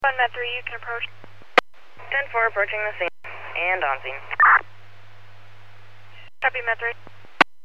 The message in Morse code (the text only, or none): none